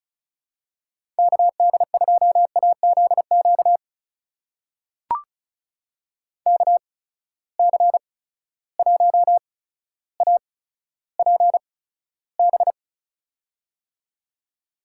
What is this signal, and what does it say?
Morse code, heard as a keyed tone, about 36 wpm, spaced as for 15 wpm, 700 Hz. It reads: KD2AZQ KC1APB
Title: USA Call Signs - 6 Characters - 8x Character Spacing - 35wpm